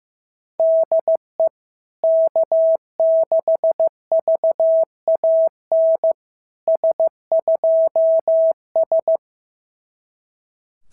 Morse code, heard as a keyed tone, about 16 wpm, 650 Hz. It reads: DE K6VAN S2S
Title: SOTA QSO Practice 15wpm